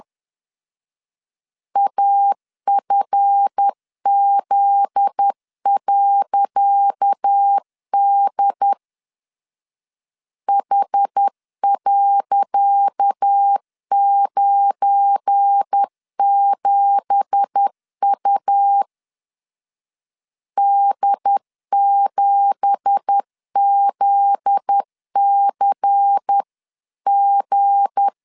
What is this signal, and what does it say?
Morse code, heard as a keyed tone, about 10 wpm, 800 Hz. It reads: AFZ.D H.97U D7ZCG
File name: LEZIONI / 10 SIGNS / 10C DOT 10wpm 2940